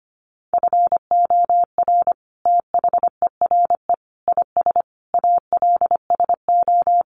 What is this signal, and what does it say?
Morse code, heard as a keyed tone, about 24 wpm, 700 Hz. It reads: FOR THERE IS ALSO
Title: Sentences from Top 1400 Words 25wpm